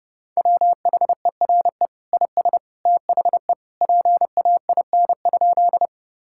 Morse code, heard as a keyed tone, about 30 wpm, 700 Hz. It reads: WHERE IS THE PAIN?
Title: Sentences from Top 1600 Words 30wpm